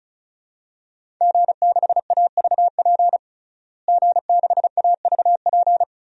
Morse code, heard as a keyed tone, about 34 wpm, 700 Hz. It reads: G6AVP G6AVP
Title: DXCC-100 Call Signs - 5 Characters - Repeated 3x - 35wpm